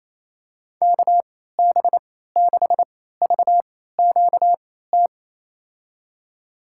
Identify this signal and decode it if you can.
Morse code, heard as a keyed tone, about 28 wpm, spaced as for 16 wpm, 700 Hz. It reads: KB6VQT
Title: USA Call Signs - 6 Characters - 3x Character Spacing - 28wpm